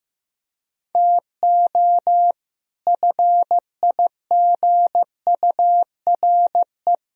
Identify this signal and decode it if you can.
Morse code, heard as a keyed tone, about 15 wpm, 700 Hz. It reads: TO FIGURE
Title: Speed-Racing - Sets of 5 Words 15wpm